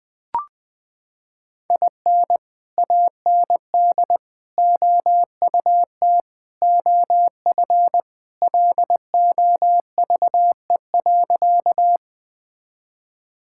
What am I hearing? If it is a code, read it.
Morse code, heard as a keyed tone, about 20 wpm, 700 Hz. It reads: IN AND OUT OF LOVE.
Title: Sentences with 4-Word Phrases - 20wpm